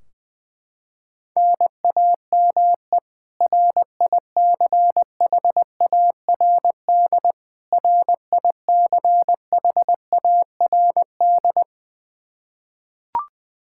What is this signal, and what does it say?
Morse code, heard as a keyed tone, about 21 wpm, 700 Hz. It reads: NAME RICHARD RICHARD
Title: Practice QSO - Exchange 1 - Beginning - 20wpm